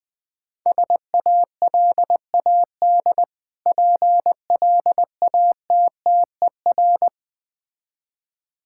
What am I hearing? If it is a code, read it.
Morse code, heard as a keyed tone, about 20 wpm, 690 Hz. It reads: SALAD PLATTER
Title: Difficult Words - Speed-Racing with Hints - 20wpm